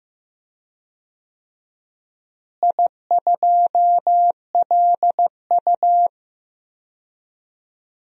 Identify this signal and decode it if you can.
Morse code, heard as a keyed tone, about 15 wpm, 700 Hz. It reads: I2LU
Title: DXCC-100 Call Signs - 4 Characters - Repeated 2x - 15wpm